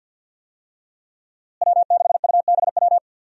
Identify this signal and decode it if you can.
Morse code, heard as a keyed tone, about 50 wpm, 700 Hz. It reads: W6UBW
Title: USA Call Signs - 5 Characters - Repeated 3x - 50wpm